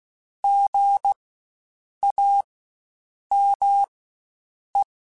Morse code, heard as a keyed tone, about 16 wpm, spaced as for 7 wpm, 800 Hz. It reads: GAME
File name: Spaced Alphas / Spaced Alphas mp3s / Gs sp @15